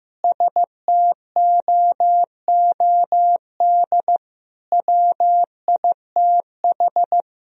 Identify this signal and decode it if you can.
Morse code, heard as a keyed tone, about 15 wpm, 700 Hz. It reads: STOOD WITH
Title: Sentences from Top 1000 Words - Review Part 2 - 15wpm